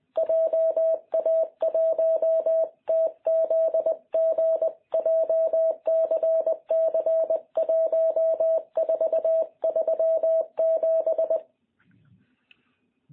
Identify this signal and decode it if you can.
Morse code, heard as a keyed tone, about 20 wpm, 650 Hz. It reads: JA1TZG JCC1437
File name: jh8cbh / 2021cw